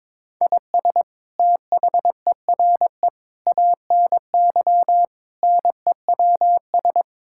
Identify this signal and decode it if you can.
Morse code, heard as a keyed tone, about 21 wpm, 700 Hz. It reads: IS THERE ANY NEWS
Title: Sentences from Top 1000 Words - Review Part 5 - 22wpm